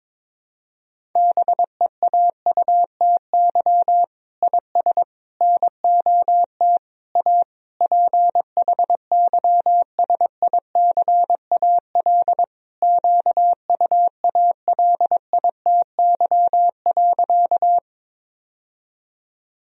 Morse code, heard as a keyed tone, about 22 wpm, 700 Hz. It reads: BEAUTY IS NOT A PHYSICAL QUALITY.
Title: Sentences from Top 1000 Words - Review Part 4 - 22wpm